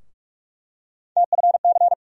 Morse code, heard as a keyed tone, about 43 wpm, 700 Hz. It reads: TFC
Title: USA Call Sign Postfixes - 3 Letters 45wpm